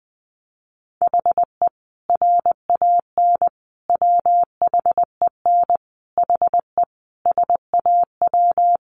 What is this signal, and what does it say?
Morse code, heard as a keyed tone, about 20 wpm, 700 Hz. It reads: HE RAN WHEN HE SAW
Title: Sentences from Top 500 Words 20wpm